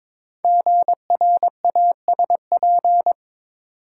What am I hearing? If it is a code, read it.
Morse code, heard as a keyed tone, about 22 wpm, 690 Hz. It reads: GRASP
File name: 5 Letter Words - First 3 Letters Given - 22wpm